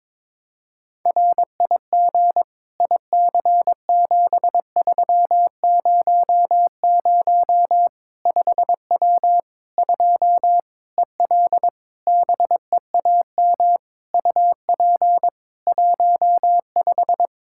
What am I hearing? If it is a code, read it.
Morse code, heard as a keyed tone, about 22 wpm, 700 Hz. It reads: RIG IC7300 5W 2 EL BEAM UP 15
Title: K1USN SST QSO Practice 22wpm